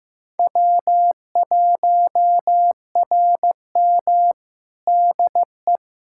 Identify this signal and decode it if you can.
Morse code, heard as a keyed tone, about 15 wpm, 700 Hz. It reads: W1RM DE
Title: Practice QSO - Exchange 4 - Ending - 15wpm